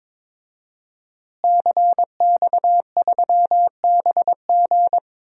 Morse code, heard as a keyed tone, about 22 wpm, 700 Hz. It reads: CX3BG
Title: DXCC-100 Call Signs - 5 Characters - Repeated 2x - 22wpm